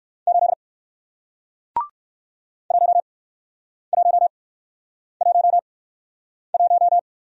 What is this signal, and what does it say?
Morse code, heard as a keyed tone, about 60 wpm, spaced as for 9 wpm, 700 Hz. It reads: 8 1111